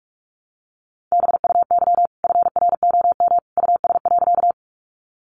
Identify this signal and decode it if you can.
Morse code, heard as a keyed tone, about 44 wpm, 700 Hz. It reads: BUY FROM US.
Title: Sentences from Top 700 Words 45wpm